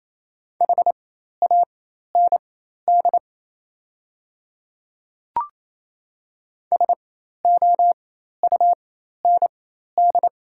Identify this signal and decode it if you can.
Morse code, heard as a keyed tone, about 28 wpm, spaced as for 13 wpm, 700 Hz. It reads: HAND SOUND
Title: 4x Character Spacing - Top 200 Words - 28wpm